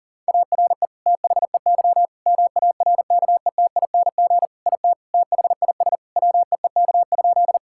Morse code, heard as a keyed tone, about 40 wpm, 700 Hz. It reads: ARE THEY MARKETING IT THIS WEEK?
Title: Sentences from Top 1000 Words - Review Part 5 - 40wpm